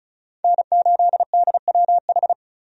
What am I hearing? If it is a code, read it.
Morse code, heard as a keyed tone, about 36 wpm, 700 Hz. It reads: N8DWH
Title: USA Call Signs - 5 Characters - 35wpm